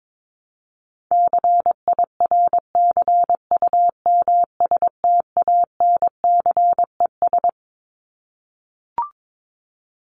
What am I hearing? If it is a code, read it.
Morse code, heard as a keyed tone, about 22 wpm, 700 Hz. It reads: CIRCUMSTANCES
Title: ES:S Suffix - Top 100 Words with Base Words 22wpm